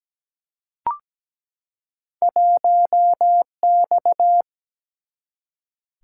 Morse code, heard as a keyed tone, about 17 wpm, 700 Hz. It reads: 1X
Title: Single-Letter Single-Number 17wpm